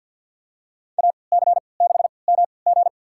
Morse code, heard as a keyed tone, about 52 wpm, spaced as for 29 wpm, 700 Hz. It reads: AC6MG
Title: USA Call Signs - 5 Characters - 3x Character Spacing - 50wpm